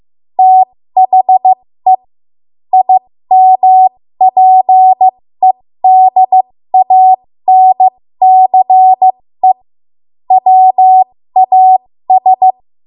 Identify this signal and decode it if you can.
Morse code, heard as a keyed tone, about 15 wpm, spaced as for 13 wpm, 750 Hz. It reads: THE IMPEDANCE WAS
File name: Morse / 13WPM